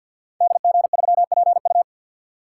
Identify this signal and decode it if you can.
Morse code, heard as a keyed tone, about 51 wpm, 700 Hz. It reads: DG3PU
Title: Code Groups - 5 Letter:Numbers - 50wpm